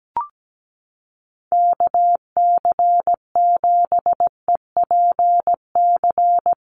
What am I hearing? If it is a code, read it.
Morse code, heard as a keyed tone, about 17 wpm, 700 Hz. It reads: KC7EPC
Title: USA Call Signs Repeated 3x 17wpm